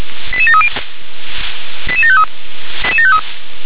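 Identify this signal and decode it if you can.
Five-tone selective call (ZVEI), heard as quick runs of five tones, tones of about 70 ms. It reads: 88730 88753
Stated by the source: ZVEI five-tone selective call received by SDR.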